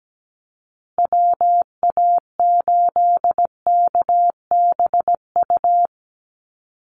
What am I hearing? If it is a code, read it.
Morse code, heard as a keyed tone, about 17 wpm, 700 Hz. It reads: WA8KBU
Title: Call Signs Repeated 3x - 17wpm